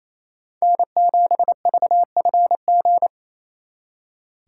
Morse code, heard as a keyed tone, about 34 wpm, 700 Hz. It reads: N7VFG
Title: Speed Racing - USA Call Signs 28wpm